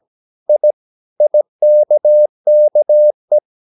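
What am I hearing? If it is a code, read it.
Morse code, heard as a keyed tone, about 17 wpm, 600 Hz. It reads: I IKKE
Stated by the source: Slow (QRS) Morse practice at 17 wpm, from CW med Gnister.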